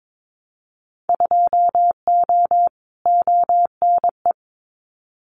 Morse code, heard as a keyed tone, about 22 wpm, 700 Hz. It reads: 2O ONE
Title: Winter Field Day Exchanges Encore 22wpm